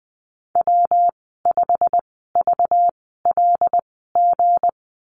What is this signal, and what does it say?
Morse code, heard as a keyed tone, about 19 wpm, spaced as for 14 wpm, 700 Hz. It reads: W5VLG
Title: 193 Lesson 31g - Letter v - Call Signs 20wpm